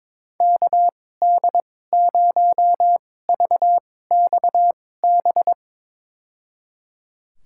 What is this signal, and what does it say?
Morse code, heard as a keyed tone, about 22 wpm, spaced as for 16 wpm, 700 Hz. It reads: KD0VXB
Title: USA Call Signs - 6 Characters - 2x Character Spacing - 22wpm